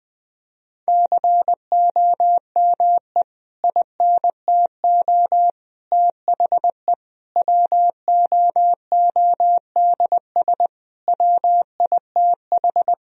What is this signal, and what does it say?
Morse code, heard as a keyed tone, about 20 wpm, 700 Hz. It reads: COME INTO THE WOODS WITH
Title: Sentences from Top 1000 Words - Review Part 2 - 20wpm